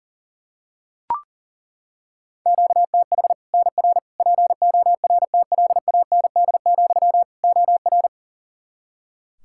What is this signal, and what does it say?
Morse code, heard as a keyed tone, about 37 wpm, 700 Hz. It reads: QTH NR PORTLAND, OR
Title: Practice QSO - Exchange 1 - Beginning - 40wpm